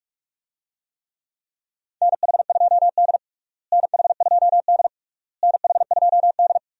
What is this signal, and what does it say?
Morse code, heard as a keyed tone, about 46 wpm, 700 Hz. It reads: NH2D NH2D NH2D
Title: USA Call Signs - 4 Characters - Repeated 3x - 45wpm